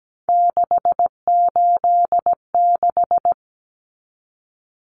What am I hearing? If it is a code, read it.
Morse code, heard as a keyed tone, about 17 wpm, 690 Hz. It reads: 686
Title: Common Cut Numbers - 3-Digit 17wpm